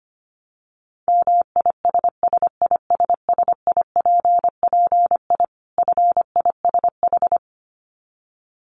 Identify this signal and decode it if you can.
Morse code, heard as a keyed tone, about 25 wpm, 700 Hz. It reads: MISSISSIPPI FISH
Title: Dit Challenge 25wpm